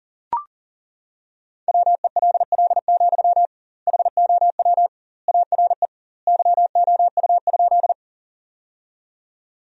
Morse code, heard as a keyed tone, about 40 wpm, 700 Hz. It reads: WELL, HOW ARE YOU?
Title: 2x Word-Spacing - Sentences from Top 200 Words - 40wpm